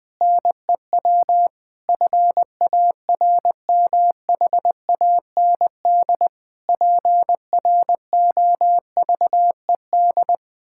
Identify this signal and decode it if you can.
Morse code, heard as a keyed tone, about 21 wpm, 700 Hz. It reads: NEW FARMHAND PROVED
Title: Pangram Challenge v2 - 20wpm